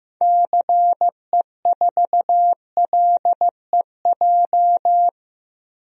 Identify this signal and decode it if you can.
Morse code, heard as a keyed tone, about 15 wpm, 700 Hz. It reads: CE4LEJ
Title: DXCC-100 Call Signs - 6 Characters - 15wpm